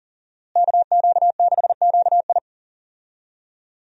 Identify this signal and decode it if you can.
Morse code, heard as a keyed tone, about 40 wpm, 700 Hz. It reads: KQ6QI
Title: USA Call Signs - 5 Characters - Repeated 2x - 40wpm